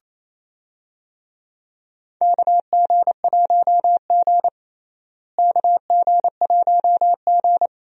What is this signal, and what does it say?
Morse code, heard as a keyed tone, about 27 wpm, 700 Hz. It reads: KG1G KG1G
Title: USA Call Signs - 4 Characters - Repeated 2x - 28wpm